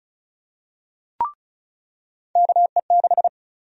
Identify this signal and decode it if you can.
Morse code, heard as a keyed tone, about 34 wpm, 700 Hz. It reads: KE6
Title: USA Call Sign Prefixes - 35wpm